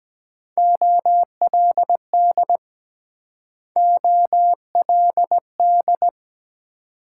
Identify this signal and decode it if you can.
Morse code, heard as a keyed tone, about 23 wpm, 700 Hz. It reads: OLD OLD